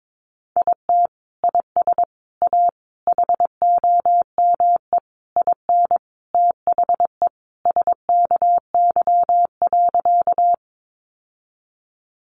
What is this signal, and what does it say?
Morse code, heard as a keyed tone, about 22 wpm, 700 Hz. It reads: IT IS A HOME IN THE SKY.
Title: Sentences from Top 1200 Words 22wpm